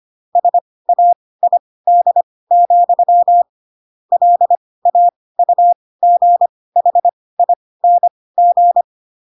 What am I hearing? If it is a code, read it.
Morse code, heard as a keyed tone, about 25 wpm, spaced as for 18 wpm, 700 Hz. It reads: SAID, LAUGHING
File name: Pinocchio / Pinocchio ch1 12 25 ik1hge